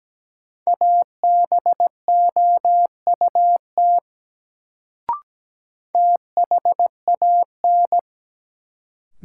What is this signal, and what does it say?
Morse code, heard as a keyed tone, about 17 wpm, 700 Hz. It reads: ABOUT THAN
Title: Sentences from Top 100 Words - New Word Drill - 17wpm